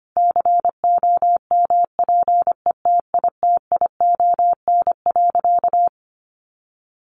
Morse code, heard as a keyed tone, about 25 wpm, 700 Hz. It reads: COMPETITION.
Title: Sentences from Top 1000 Words 25wpm